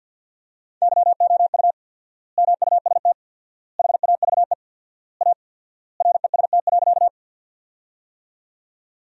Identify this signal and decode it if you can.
Morse code, heard as a keyed tone, about 48 wpm, 700 Hz. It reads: YOU MUST HAVE A REST.
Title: 4x Word-Spacing - Sentences from Top 500 Words - 50wpm